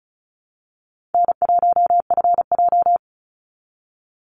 Morse code, heard as a keyed tone, about 34 wpm, 700 Hz. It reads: N1FJ
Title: NA SOTA Chaser Callsigns 35wpm